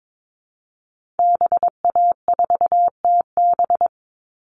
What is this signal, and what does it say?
Morse code, heard as a keyed tone, about 22 wpm, 700 Hz. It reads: BA4TB
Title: Call Signs Repeated 2x - 22wpm